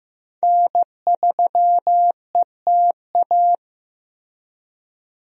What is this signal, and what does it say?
Morse code, heard as a keyed tone, about 14 wpm, 700 Hz. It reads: N3ETA
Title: USA Call Signs - 5 Characters - Repeated 2x - 15wpm